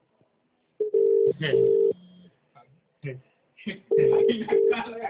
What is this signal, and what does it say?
Ringback tone (UK), 400 + 450 Hz, with speech.